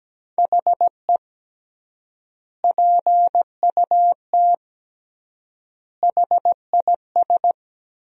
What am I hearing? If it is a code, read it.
Morse code, heard as a keyed tone, about 17 wpm, 700 Hz. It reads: HE PUT HIS
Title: 3x Word-Spacing - Sentences from Top 400 Words - 17wpm